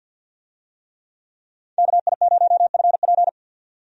Morse code, heard as a keyed tone, about 52 wpm, 700 Hz. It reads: KI0FP
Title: USA Call Signs - 5 Characters - 50wpm